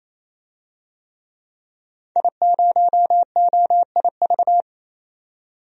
Morse code, heard as a keyed tone, about 27 wpm, 700 Hz. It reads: I0OIV
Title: DXCC-100 Call Signs - 5 Characters - 28wpm